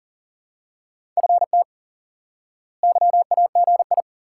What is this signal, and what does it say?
Morse code, heard as a keyed tone, about 40 wpm, 700 Hz. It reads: FT YAGI
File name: course / instant-qso-element-course-3 / IQE Course 3 - Lesson 89 - YAGI - Two Elements - 40wpm